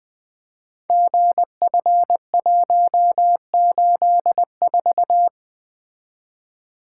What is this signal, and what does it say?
Morse code, heard as a keyed tone, about 20 wpm, 700 Hz. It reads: GF184
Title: Code Groups - 5 Letter:Numbers - 20wpm